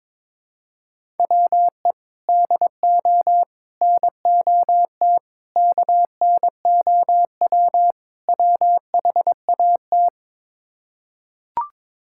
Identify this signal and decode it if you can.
Morse code, heard as a keyed tone, about 22 wpm, 700 Hz. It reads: WE DO NOT KNOW WHAT
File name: practice / 22wpm / Build A Phrase - 5-Word Phrases - 22wpm